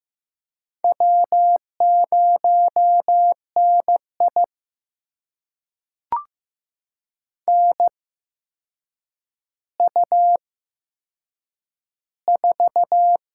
Morse code, heard as a keyed tone, about 15 wpm, spaced as for 7 wpm, 700 Hz. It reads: W0NI NU4
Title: USA Call Signs - 4 Characters - 8x Character Spacing - 15wpm